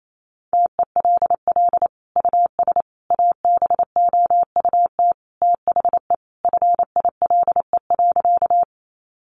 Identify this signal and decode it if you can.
Morse code, heard as a keyed tone, about 28 wpm, 700 Hz. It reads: TELL US ABOUT THE FILE.
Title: Sentences from Top 1200 Words 28wpm